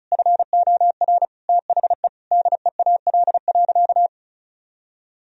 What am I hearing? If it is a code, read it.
Morse code, heard as a keyed tone, about 36 wpm, 700 Hz. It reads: FOR THE DEAL.
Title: Sentences from Top 1000 Words - Review Part 4 - 35wpm